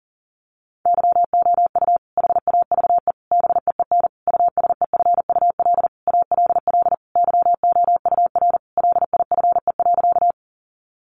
Morse code, heard as a keyed tone, about 40 wpm, 700 Hz. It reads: YOU HAVE BEEN USEFUL ALL YOUR LIFE.
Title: Sentences from Top 700 Words 40wpm